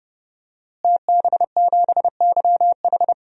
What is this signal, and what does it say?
Morse code, heard as a keyed tone, about 29 wpm, 700 Hz. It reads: TB7YH
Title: DXCC-100 Call Signs - 5 Characters - Repeated 3x - 30wpm